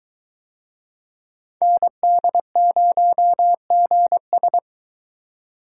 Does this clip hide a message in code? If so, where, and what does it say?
Yes: Morse code from 1.6 to 4.6 s, heard as a keyed tone, about 23 wpm, 700 Hz. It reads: ND0GS